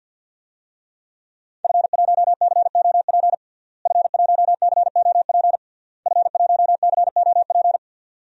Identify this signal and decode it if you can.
Morse code, heard as a keyed tone, about 52 wpm, 700 Hz. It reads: F1COP F1COP F1COP
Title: DXCC-100 Call Signs - 5 Characters - Repeated 3x - 50wpm